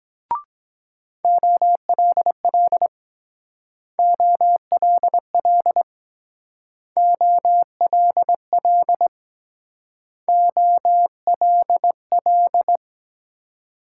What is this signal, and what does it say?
Morse code, heard as a keyed tone, about 21 wpm, 700 Hz. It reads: OLL OLL OLL OLL